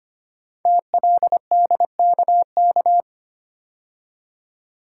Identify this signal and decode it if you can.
Morse code, heard as a keyed tone, about 25 wpm, 700 Hz. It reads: TLDKK
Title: Code Groups - 5 Letters - 25wpm